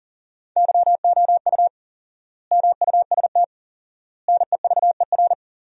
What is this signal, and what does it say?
Morse code, heard as a keyed tone, about 40 wpm, 700 Hz. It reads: YOU MUST NEVER
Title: 4x Word-Spacing - Sentences from Top 300 Words - 40wpm